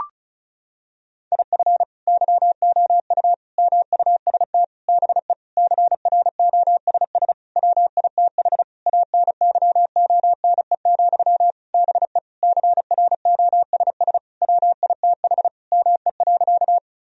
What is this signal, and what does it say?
Morse code, heard as a keyed tone, about 35 wpm, 700 Hz. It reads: IF YOU MUST BE CROSS WITH ANYONE, BE CROSS WITH ME.
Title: Sentences from Top 1000 Words - Review Part 1 - 35wpm